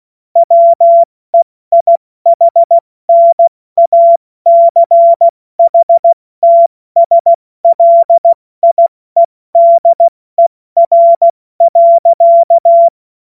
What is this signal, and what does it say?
Morse code, heard as a keyed tone, about 16 wpm, spaced as for 14 wpm, 680 Hz. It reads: WEIHNACHTSLIEDER.
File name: Morsekurs / Witze / Witz 25. April 70 F80